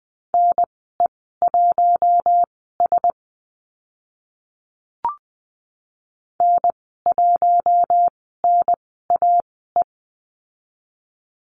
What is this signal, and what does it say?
Morse code, heard as a keyed tone, about 20 wpm, spaced as for 15 wpm, 700 Hz. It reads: NE1S N1NAE
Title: 048 Lesson 9g - Number 4 - Call Signs 20wpm